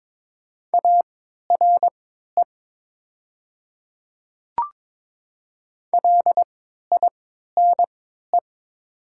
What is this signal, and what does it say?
Morse code, heard as a keyed tone, about 22 wpm, spaced as for 13 wpm, 700 Hz. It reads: ARE LINE
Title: 3x Character Spacing - Top 300 Words - 22wpm